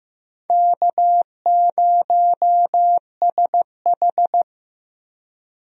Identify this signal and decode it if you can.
Morse code, heard as a keyed tone, about 15 wpm, 700 Hz. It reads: K0SH
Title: USA Call Signs - 4 Characters - 15wpm